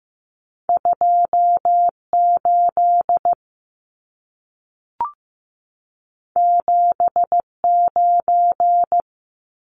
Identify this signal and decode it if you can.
Morse code, heard as a keyed tone, about 15 wpm, 700 Hz. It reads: 28 79
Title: Numbers - 2-Digit 15wpm